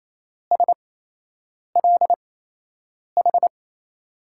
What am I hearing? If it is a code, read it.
Morse code, heard as a keyed tone, about 28 wpm, spaced as for 8 wpm, 700 Hz. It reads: SLH
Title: USA Call Signs - 8x Character Spacing - 28wpm